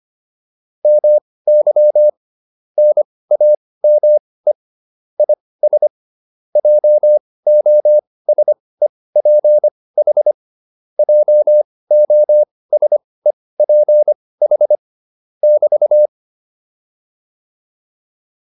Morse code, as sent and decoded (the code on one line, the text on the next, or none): -- -.-- / -. .- -- . / .. ... / .--- --- ... .--. .... / .--- --- ... .--. .... / -...-
MY NAME IS JOSEPH JOSEPH <BT>